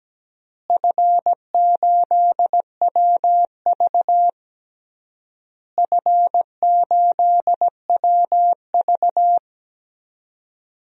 Morse code, heard as a keyed tone, about 17 wpm, 700 Hz. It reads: F8WV F8WV
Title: DXCC-100 Call Signs - 4 Characters - Repeated 3x - 17wpm